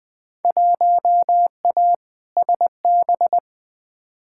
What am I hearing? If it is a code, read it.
Morse code, heard as a keyed tone, about 20 wpm, 700 Hz. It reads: 1A SB
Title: Field Day Exchanges Encore - 20wpm